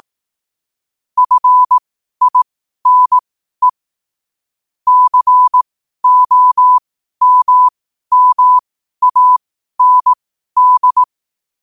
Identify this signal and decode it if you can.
Morse code, heard as a keyed tone, about 18 wpm, spaced as for 12 wpm, 1000 Hz. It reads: FINE COMMAND
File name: qotdfiles / 12wpm